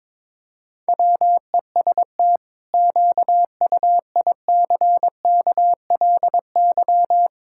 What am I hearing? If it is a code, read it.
Morse code, heard as a keyed tone, about 22 wpm, 700 Hz. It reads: WEST QUICKLY
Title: Pangram Challenge v2 - 22wpm